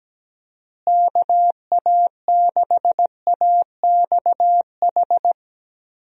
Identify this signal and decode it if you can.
Morse code, heard as a keyed tone, about 17 wpm, 700 Hz. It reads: KA6AXH
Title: USA Call Signs - 6 Characters - Repeated 3x - 17wpm